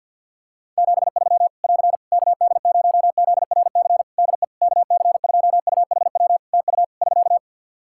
Morse code, heard as a keyed tone, about 52 wpm, 700 Hz. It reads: 73 <AR> KD0ZRC DE KC2UHW TU <SK>
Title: Practice QSO - Exchange 4 - Ending - 50wpm